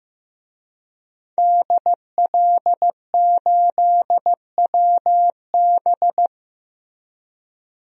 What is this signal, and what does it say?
Morse code, heard as a keyed tone, about 15 wpm, 700 Hz. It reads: DL8WB
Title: DXCC-100 Call Signs - 5 Characters - Repeated 2x - 15wpm